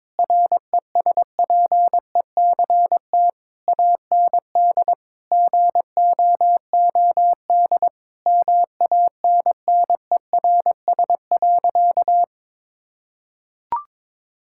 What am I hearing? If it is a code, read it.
Morse code, heard as a keyed tone, about 22 wpm, 700 Hz. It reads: RESPECT AND GOOD MANNERS.
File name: Difficult Sentences with Hints - 22wpm